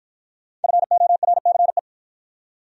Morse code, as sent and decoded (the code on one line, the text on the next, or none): ..-. --- .-. -.-. .
FORCE